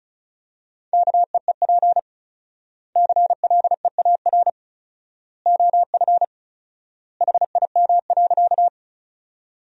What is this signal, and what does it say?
Morse code, heard as a keyed tone, about 36 wpm, 700 Hz. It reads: KEEP CLEAR OF HIM.
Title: 4x Word-Spacing - Sentences from Top 500 Words - 35wpm